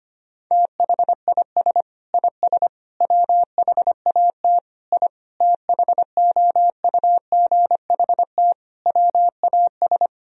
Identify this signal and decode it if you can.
Morse code, heard as a keyed tone, about 25 wpm, 700 Hz. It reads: THIS IS WHAT I THOUGHT WAS
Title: Sentences from Top 1000 Words - Review Part 4 - 25wpm